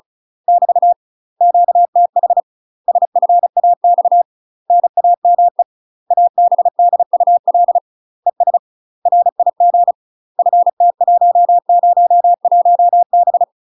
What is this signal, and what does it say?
Morse code, heard as a keyed tone, about 36 wpm, 700 Hz. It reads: <BT> QTH SFAX NAME ABDUL ES RIG FT101B
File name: UK MASTER QSOs TRAINING / 35WPM / UK MASTER QSOs 0020